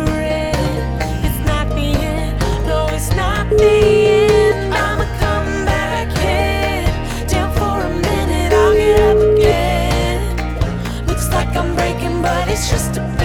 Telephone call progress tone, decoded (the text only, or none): ringback tone